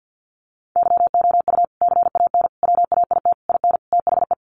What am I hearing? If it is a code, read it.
Morse code, heard as a keyed tone, about 48 wpm, 700 Hz. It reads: YOU CAN WAIT IN THE
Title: Sentences from Top 1400 Words 50wpm